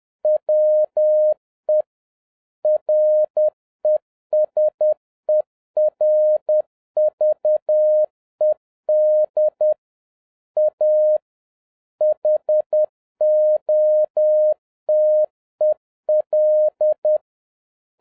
Morse code, heard as a keyed tone, about 10 wpm, 610 Hz. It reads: WE RESERVED A HOTEL